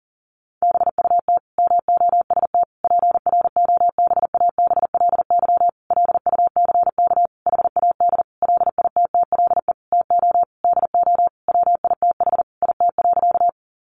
Morse code, heard as a keyed tone, about 40 wpm, 700 Hz. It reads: BUT MOST PROBABLY LUCK HAD LITTLE TO DO WITH IT.
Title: Sentences from Top 1500 Words 40wpm